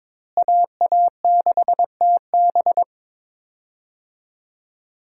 Morse code, heard as a keyed tone, about 22 wpm, 700 Hz. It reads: AA6TB
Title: USA Call Signs - 5 Characters - Repeated 2x - 22wpm